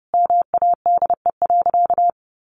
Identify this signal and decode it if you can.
Morse code, heard as a keyed tone, about 30 wpm, 700 Hz. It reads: MADE.
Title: Sentences from Top 1400 Words 30wpm